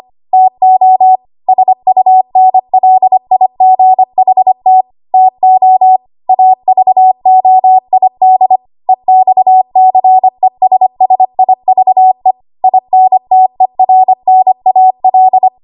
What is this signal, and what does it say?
Morse code, heard as a keyed tone, about 25 wpm, 750 Hz. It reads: TO SUNLIGHT TO AVOID EXCESSIVE INTERNAL
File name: Archive / 25 WPM / 25WPM